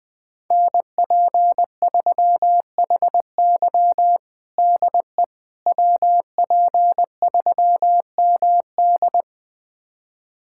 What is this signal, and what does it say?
Morse code, heard as a keyed tone, about 21 wpm, 700 Hz. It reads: NP3HY DE WP3MD
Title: USA Callsign DE Callsign - 5 Characters - 20wpm